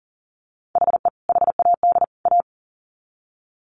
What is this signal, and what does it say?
Morse code, heard as a keyed tone, about 40 wpm, 700 Hz. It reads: HE HAD A
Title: Sets of 3 Words 40wpm